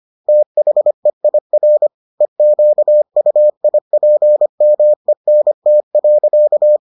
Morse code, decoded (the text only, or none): THEIR EQUIPMENT.